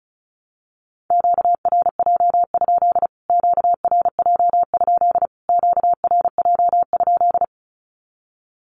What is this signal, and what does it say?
Morse code, heard as a keyed tone, about 34 wpm, 700 Hz. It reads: QRJ? QRJ? QRJ?
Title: Q-Codes Repeated 3x 35wpm